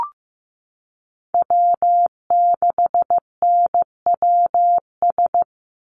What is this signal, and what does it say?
Morse code, heard as a keyed tone, about 15 wpm, 700 Hz. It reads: W6NWS
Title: Call Signs Repeated 2x - 15wpm